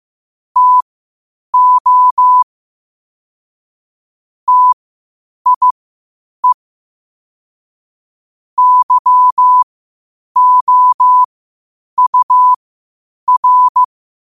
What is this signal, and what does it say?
Morse code, heard as a keyed tone, about 15 wpm, spaced as for 8 wpm, 1000 Hz. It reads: TO TIE YOUR
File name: qotdfiles / 8wpm